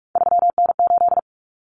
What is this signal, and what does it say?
Morse code, heard as a keyed tone, about 44 wpm, 700 Hz. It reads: 3N8
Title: Common Cut Numbers - 3-Digit 45wpm